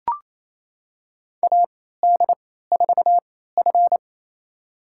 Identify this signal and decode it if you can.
Morse code, heard as a keyed tone, about 27 wpm, spaced as for 16 wpm, 700 Hz. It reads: AD4F
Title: USA Call Signs - 3x Character Spacing - 28wpm